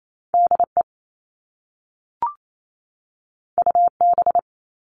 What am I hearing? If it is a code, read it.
Morse code, heard as a keyed tone, about 28 wpm, 700 Hz. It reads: DE UB
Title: All Cut Numbers - 2-Digit 28wpm